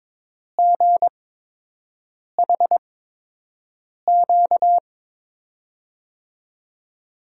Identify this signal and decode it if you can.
Morse code, heard as a keyed tone, about 22 wpm, spaced as for 6 wpm, 700 Hz. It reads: GHQ
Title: USA Call Signs - 5 Characters - 8x Character Spacing - 22wpm